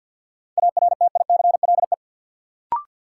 Morse code, heard as a keyed tone, about 49 wpm, 700 Hz. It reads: ARTICLE
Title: Sentences from Top 1000 Words - Review Part 5 - New Word Drill - 50wpm